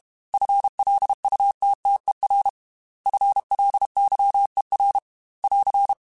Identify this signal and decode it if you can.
Morse code, heard as a keyed tone, about 32 wpm, 800 Hz. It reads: FLUTTER FLYER <AR>
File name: FL Runs / FL Runs mp3s / FL Runs @30